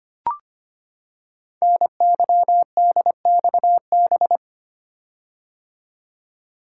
Morse code, heard as a keyed tone, about 25 wpm, 700 Hz. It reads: NYDXB
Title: Code Groups - 5 Letters - 25wpm